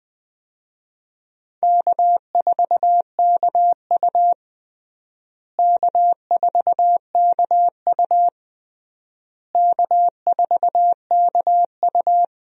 Morse code, heard as a keyed tone, about 20 wpm, 700 Hz. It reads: K4KU K4KU K4KU
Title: USA Call Signs - 4 Characters - Repeated 3x - 20wpm